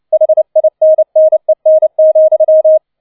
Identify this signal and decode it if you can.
Morse code, heard as a keyed tone, about 29 wpm, 630 Hz. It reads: HINNEN,